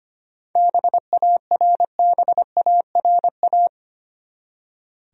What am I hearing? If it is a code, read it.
Morse code, heard as a keyed tone, about 25 wpm, 700 Hz. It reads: BARBARA
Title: USA Common Names - 25wpm